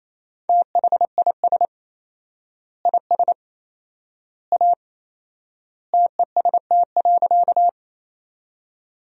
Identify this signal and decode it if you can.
Morse code, heard as a keyed tone, about 28 wpm, 700 Hz. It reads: THIS IS A TEST.
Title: 4x Word-Spacing - Sentences from Top 500 Words - 28wpm